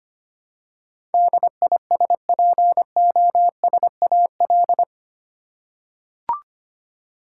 Morse code, heard as a keyed tone, about 25 wpm, 700 Hz. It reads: DISPOSAL
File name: Eight Letter Words - 25wpm